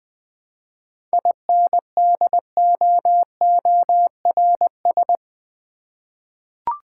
Morse code, heard as a keyed tone, about 20 wpm, 700 Hz. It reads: INDOORS
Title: 7 Letter Words - First 3 Letters Given - 20wpm